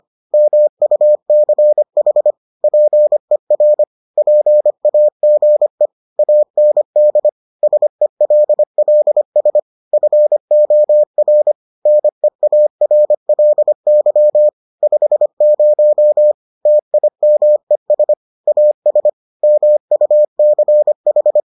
Morse code, heard as a keyed tone, about 25 wpm, 600 Hz. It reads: MUCH PER PAGE AND SELLS FOR NEARLY 50 TIMES AS MUCH